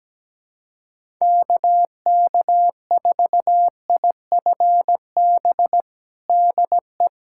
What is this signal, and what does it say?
Morse code, heard as a keyed tone, about 17 wpm, 700 Hz. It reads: KK4IFB DE
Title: USA Callsign DE Callsign - 6 Characters - 17wpm